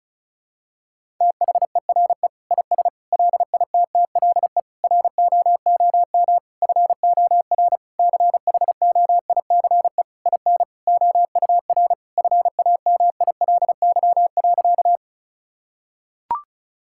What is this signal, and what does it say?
Morse code, heard as a keyed tone, about 34 wpm, 700 Hz. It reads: THERE IS LITTLE ROOM FOR CHOICE IN OUR FAMILY.
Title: Sentences from Top 1000 Words - Review Part 6 - 35wpm